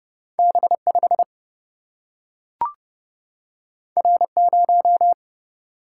Morse code, heard as a keyed tone, about 31 wpm, 700 Hz. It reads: B 5 R 0
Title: Single-Letter Single-Number 30wpm